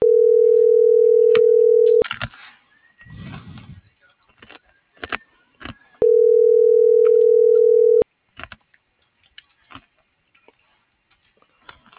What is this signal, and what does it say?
Ringback tone (North America), 440 + 480 Hz.